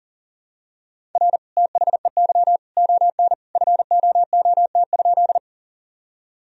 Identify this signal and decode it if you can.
Morse code, heard as a keyed tone, about 40 wpm, 700 Hz. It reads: R THEY ON FOOT?
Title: Sentences from Top 500 Words and Abbreviations 40wpm